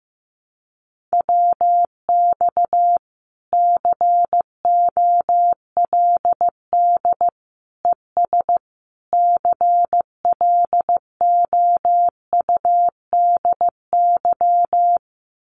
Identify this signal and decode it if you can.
Morse code, heard as a keyed tone, about 15 wpm, 700 Hz. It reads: WX COLD ES CLOUDY
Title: Common QSO Phrases Encore 15wpm